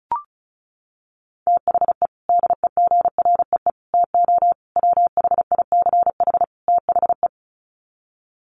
Morse code, heard as a keyed tone, about 34 wpm, 700 Hz. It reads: THE DEGREE TO WHICH THE
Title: Sets of 5 Words Encore2 35wpm